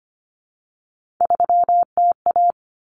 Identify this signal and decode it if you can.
Morse code, heard as a keyed tone, about 25 wpm, 700 Hz. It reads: 3TA
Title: Common Cut Numbers - 3-Digit 25wpm